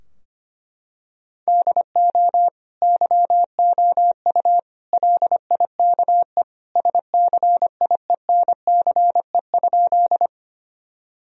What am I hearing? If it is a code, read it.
Morse code, heard as a keyed tone, about 25 wpm, 700 Hz. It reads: DO YOU LIKE SCIENCE?
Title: Sentences from Top 1000 Words - Review Part 2 - 25wpm